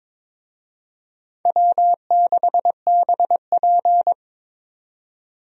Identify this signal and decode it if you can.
Morse code, heard as a keyed tone, about 21 wpm, 700 Hz. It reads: W6BP
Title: USA Call Signs - 4 Characters - 22wpm